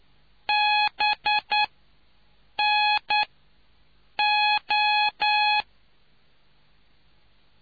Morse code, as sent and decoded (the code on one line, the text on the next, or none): -... -. ---
BNO